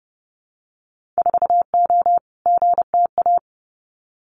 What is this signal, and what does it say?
Morse code, heard as a keyed tone, about 30 wpm, 700 Hz. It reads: 4O GTA
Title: Winter Field Day Exchanges 30wpm